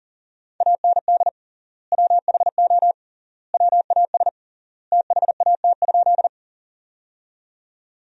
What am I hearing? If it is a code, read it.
Morse code, heard as a keyed tone, about 37 wpm, 700 Hz. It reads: AND WHO WAS THAT?
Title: 3x Word-Spacing - Sentences from Top 100 Words - 40wpm